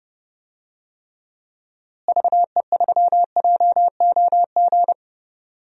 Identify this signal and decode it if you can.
Morse code, heard as a keyed tone, about 29 wpm, 700 Hz. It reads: VE3JOG
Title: DXCC-100 Call Signs - 6 Characters - Repeated 3x - 30wpm